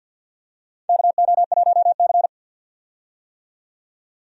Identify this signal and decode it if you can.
Morse code, heard as a keyed tone, about 51 wpm, 700 Hz. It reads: KO1C
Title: USA Call Signs - 4 Characters - 50wpm